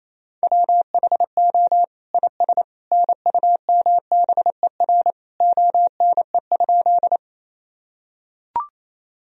Morse code, heard as a keyed tone, about 28 wpm, 700 Hz. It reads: WHO IS NUMBER ONE?
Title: Sentences from Top 1000 Words - Review Part 1 - 28wpm